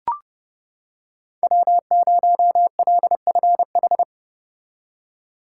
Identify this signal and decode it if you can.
Morse code, heard as a keyed tone, about 30 wpm, 700 Hz. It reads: W0LFH